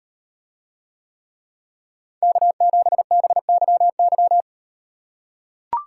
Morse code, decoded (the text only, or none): K7BYY